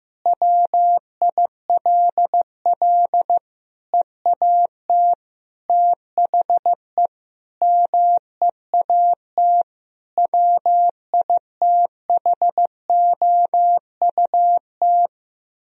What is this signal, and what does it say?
Morse code, heard as a keyed tone, about 15 wpm, 700 Hz. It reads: WILL EAT THE MEAT WITHOUT
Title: Sentences from Top 1000 Words - Review Part 3 - 15wpm